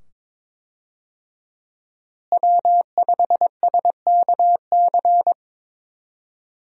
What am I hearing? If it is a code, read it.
Morse code, heard as a keyed tone, about 22 wpm, 700 Hz. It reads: W5SKC
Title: USA Call Signs - 5 Characters - Repeated 2x - 22wpm